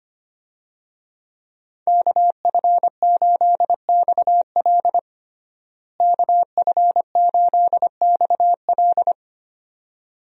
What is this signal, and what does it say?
Morse code, heard as a keyed tone, about 25 wpm, 700 Hz. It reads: KF8XL KF8XL
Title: USA Call Signs - 5 Characters - Repeated 2x - 25wpm